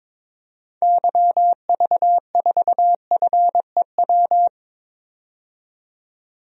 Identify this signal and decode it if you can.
Morse code, heard as a keyed tone, about 22 wpm, 700 Hz. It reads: YV4FEW
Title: DXCC-100 Call Signs - 6 Characters - 22wpm